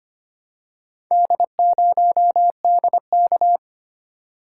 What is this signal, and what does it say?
Morse code, heard as a keyed tone, about 25 wpm, 700 Hz. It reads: D0DK
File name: practice / 25wpm / Four Letter-Number Combinations - 25wpm